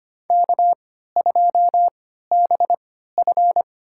Morse code, heard as a keyed tone, about 25 wpm, spaced as for 14 wpm, 700 Hz. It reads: K2BF
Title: USA Call Signs - 4 Characters - 3x Character Spacing - 25wpm